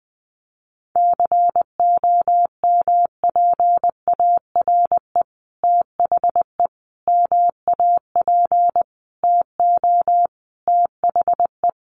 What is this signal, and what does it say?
Morse code, heard as a keyed tone, about 20 wpm, 700 Hz. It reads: COMPARE THE MAP TO THE
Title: Sentences from Top 1300 Words 20wpm